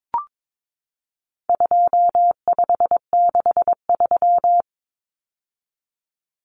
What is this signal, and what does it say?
Morse code, heard as a keyed tone, about 22 wpm, 700 Hz. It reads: 2563
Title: Common Cut Numbers - 4-Digit 22wpm